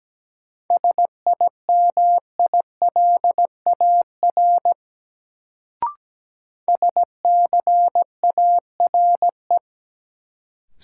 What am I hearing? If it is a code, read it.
Morse code, heard as a keyed tone, about 17 wpm, 700 Hz. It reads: SIMILAR SCARE
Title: Sentences from Top 1000 Words - Review Part 5 - New Word Drill - 17wpm